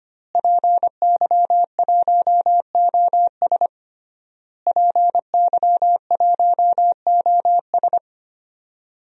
Morse code, heard as a keyed tone, about 25 wpm, 700 Hz. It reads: PY1OS PY1OS
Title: DXCC-100 Call Signs - 5 Characters - Repeated 3x - 25wpm